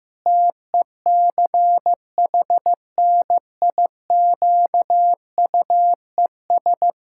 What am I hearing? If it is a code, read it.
Morse code, heard as a keyed tone, about 16 wpm, 700 Hz. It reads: TECHNIQUES
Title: Pangram Challenge v2 - 15wpm